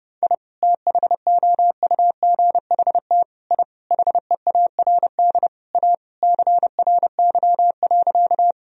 Morse code, heard as a keyed tone, about 30 wpm, 700 Hz. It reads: I THOUGHT I HEARD A CRY.
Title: Sentences from Top 1000 Words - Review Part 2 - 30wpm